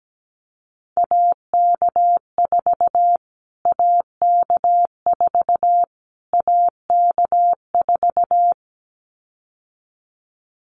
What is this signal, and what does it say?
Morse code, heard as a keyed tone, about 17 wpm, 700 Hz. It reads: AK4 AK4 AK4
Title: Callsign Prefixes for Top 25 DXCC - Repeated 3x - 17wpm